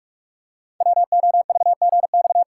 Morse code, heard as a keyed tone, about 44 wpm, 700 Hz. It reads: WOVGX
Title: Code Groups - 5 Letters - 45wpm